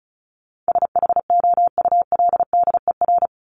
Morse code, heard as a keyed tone, about 34 wpm, 700 Hz. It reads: SHOULDER
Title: Top 1501-1750 Words 35wpm